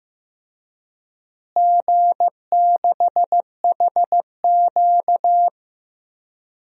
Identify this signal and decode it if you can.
Morse code, heard as a keyed tone, about 15 wpm, 700 Hz. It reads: G6HQ
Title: DXCC-100 Call Signs - 4 Characters - Repeated 2x - 15wpm